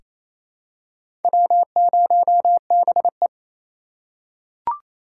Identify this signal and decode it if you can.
Morse code, heard as a keyed tone, about 28 wpm, 700 Hz. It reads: W0BE
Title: USA Call Signs - 2x Character Spacing - 28wpm